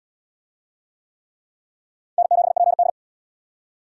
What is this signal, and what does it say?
Morse code, heard as a keyed tone, about 70 wpm, 700 Hz. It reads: N7PD